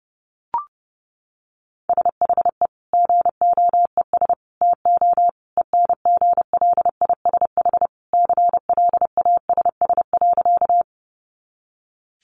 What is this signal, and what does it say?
Morse code, heard as a keyed tone, about 30 wpm, 700 Hz. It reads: SHE GOES TO ENGLISH CLASS.
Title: Additional Easy Phrases 30wpm